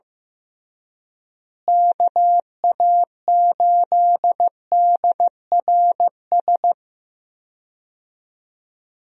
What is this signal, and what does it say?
Morse code, heard as a keyed tone, about 15 wpm, 700 Hz. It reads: KA8DRS
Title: USA Call Signs - 6 Characters - Repeated 2x - 15wpm